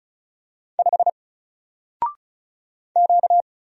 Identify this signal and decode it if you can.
Morse code, heard as a keyed tone, about 36 wpm, 700 Hz. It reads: 5 Q
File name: Single-Letter-Number-35wpm